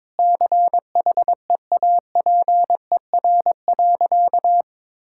Morse code, heard as a keyed tone, about 22 wpm, 700 Hz. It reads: CHEAPER.
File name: Difficult Sentences with Hints - 22wpm